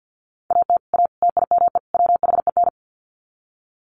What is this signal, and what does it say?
Morse code, heard as a keyed tone, about 50 wpm, 700 Hz. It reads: AT A TIME WHEN
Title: Sets of 4 Words 50wpm